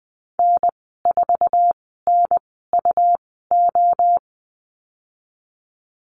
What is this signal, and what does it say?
Morse code, heard as a keyed tone, about 20 wpm, spaced as for 15 wpm, 700 Hz. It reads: N4NUO
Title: 140 Lesson 23g - Letter f - Call Signs 20wpm